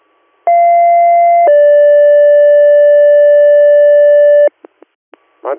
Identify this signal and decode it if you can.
Two-tone page, heard as a one-second tone, then a three-second tone, about 690 then 580 Hz, with speech. It is heard from 0.5 to 4.5 s.